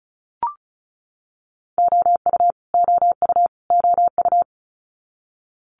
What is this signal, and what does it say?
Morse code, heard as a keyed tone, about 36 wpm, 700 Hz. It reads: OU OU OU
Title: Multi-letter Phonemes Repeated 3x 35wpm